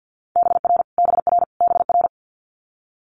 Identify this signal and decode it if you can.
Morse code, heard as a keyed tone, about 50 wpm, 690 Hz. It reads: BR BR BR
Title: Multi-letter Phonemes Repeated 3x 50wpm